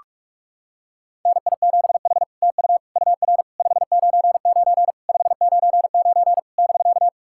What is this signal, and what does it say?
Morse code, heard as a keyed tone, about 44 wpm, 660 Hz. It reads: NI7H TU UR 599 599 <BK>
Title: SOTA QSO Practice 45wpm